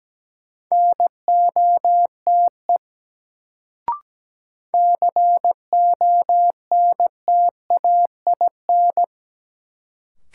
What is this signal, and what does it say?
Morse code, heard as a keyed tone, about 17 wpm, 700 Hz. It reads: NOTE CONTAIN
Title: Sentences from Top 500 Words - New Word Drill - 17wpm